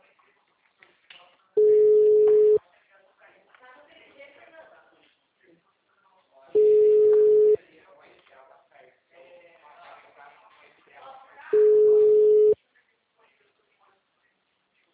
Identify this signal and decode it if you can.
Ringback tone (Europe, 425 Hz), with speech.